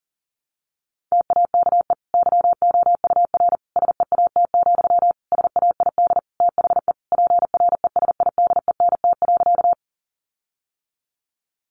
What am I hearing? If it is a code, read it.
Morse code, heard as a keyed tone, about 40 wpm, 700 Hz. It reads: TAKE YOUR SEAT, SAID THE PRESIDENT.
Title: Sentences from Top 1500 Words 40wpm